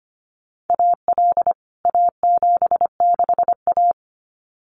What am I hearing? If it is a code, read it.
Morse code, heard as a keyed tone, about 25 wpm, 700 Hz. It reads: AL A76A
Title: CWT Exchange 25wpm